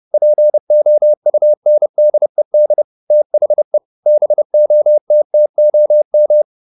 Morse code, heard as a keyed tone, about 30 wpm, 600 Hz. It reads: POUNDED THE BOTTOM